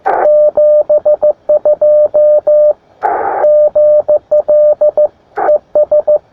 Morse code, heard as a keyed tone, about 15 wpm, 600 Hz, with radio noise. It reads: 72 GL ES
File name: attachments / sp4ipl 20m